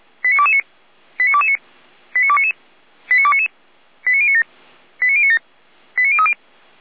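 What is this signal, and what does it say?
Five-tone selective call (ZVEI), heard as quick runs of five tones, tones of about 70 ms. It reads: 89229 89229 89299 89299 89097 89097 89030
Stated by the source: ZVEI five-tone selective call received by SDR, logged by Feuerwehr Orlamünde.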